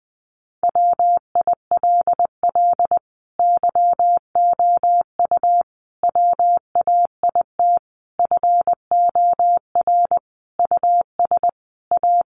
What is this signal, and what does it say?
Morse code, heard as a keyed tone, about 20 wpm, 700 Hz. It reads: WILL YOU WAIT FOR US A
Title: Sentences from Top 900 Words 20wpm